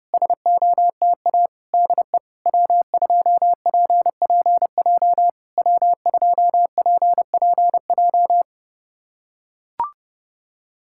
Morse code, heard as a keyed tone, about 30 wpm, 700 Hz. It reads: SOTA DE W2PPJ W2PPJ
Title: SOTA QSO Practice 30wpm